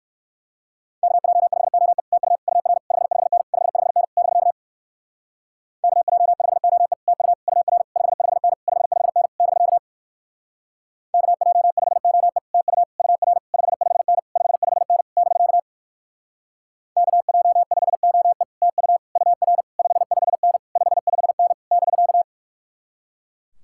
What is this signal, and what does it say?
Morse code, heard as a keyed tone, about 55 wpm, 700 Hz. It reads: KJ5OE TU UR 55N 55N <BK> KJ5OE TU UR 55N 55N <BK> KJ5OE TU UR 55N 55N <BK> KJ5OE TU UR 55N 55N <BK>
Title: Speed-Racing Common QSO Phrases - 45wpm